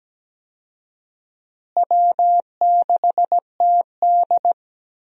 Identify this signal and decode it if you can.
Morse code, heard as a keyed tone, about 17 wpm, 700 Hz. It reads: W6TD
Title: USA Call Signs - 4 Characters - Repeated 2x - 17wpm